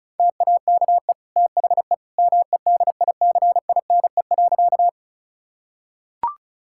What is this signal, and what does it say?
Morse code, heard as a keyed tone, about 36 wpm, 700 Hz. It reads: TAKE THE MEDICINE.
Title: Sentences from Top 1000 Words - Review Part 6 - 35wpm